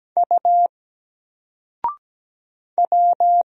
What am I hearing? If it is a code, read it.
Morse code, heard as a keyed tone, about 17 wpm, 700 Hz. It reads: U W
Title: Single Character (Letter:Number:Punctuation:Prosign) 17wpm